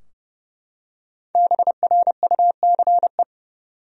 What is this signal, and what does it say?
Morse code, heard as a keyed tone, about 30 wpm, 700 Hz. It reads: BRUCE
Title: USA Common Names - 30wpm